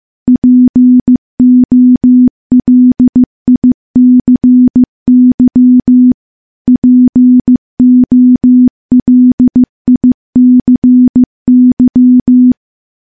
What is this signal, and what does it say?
Morse code, heard as a keyed tone, about 15 wpm, 250 Hz. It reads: POLICY POLICY
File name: morse-training-15WPM